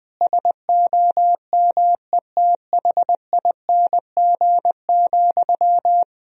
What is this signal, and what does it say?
Morse code, heard as a keyed tone, about 20 wpm, 700 Hz. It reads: SOMETHING,
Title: Sentences from Top 1000 Words - Review Part 3 - 20wpm